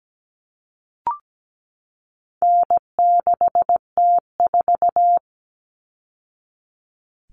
Morse code, heard as a keyed tone, about 17 wpm, 700 Hz. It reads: N6T4
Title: All Cut Numbers - 4-Digit 17wpm